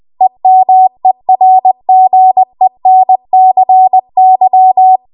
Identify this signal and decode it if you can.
Morse code, heard as a keyed tone, about 20 wpm, 750 Hz. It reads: EMERGENCY
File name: Morse / 20WPM